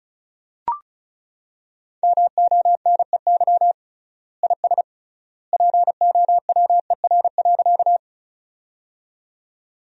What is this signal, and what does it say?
Morse code, heard as a keyed tone, about 34 wpm, 700 Hz. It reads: MONEY IS POWER.
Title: 3x Word-Spacing - Sentences from Top 500 Words - 35wpm